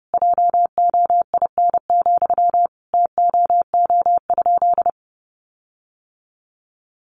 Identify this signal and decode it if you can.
Morse code, heard as a keyed tone, about 30 wpm, 700 Hz. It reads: JOIN, TOO?
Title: Sentences from Top 1400 Words 30wpm